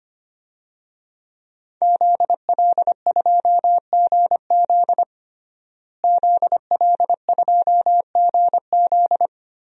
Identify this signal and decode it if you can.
Morse code, heard as a keyed tone, about 25 wpm, 700 Hz. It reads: ZL2GZ ZL2GZ
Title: DXCC-100 Call Signs - 5 Characters - Repeated 3x - 25wpm